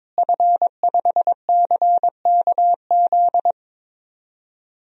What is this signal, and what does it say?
Morse code, heard as a keyed tone, about 22 wpm, 700 Hz. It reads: F5CKZ
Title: DXCC-100 Call Signs - 5 Characters - Repeated 2x - 22wpm